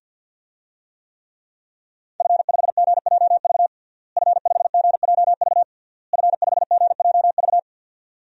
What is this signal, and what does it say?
Morse code, heard as a keyed tone, about 46 wpm, 700 Hz. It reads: F5GJV F5GJV F5GJV
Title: DXCC-100 Call Signs - 5 Characters - Repeated 3x - 50wpm